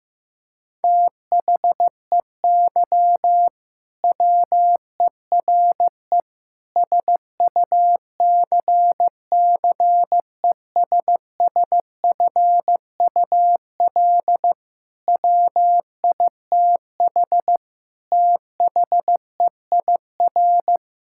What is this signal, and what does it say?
Morse code, heard as a keyed tone, about 16 wpm, 700 Hz. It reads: THEY WERE SUCCESSFUL WITH THEIR
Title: Sentences from Top 1000 Words - Review Part 6 - 15wpm